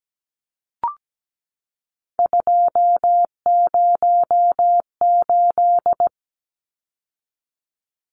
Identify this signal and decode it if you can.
Morse code, heard as a keyed tone, about 17 wpm, 700 Hz. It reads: 208
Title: Numbers - 3-Digit 17wpm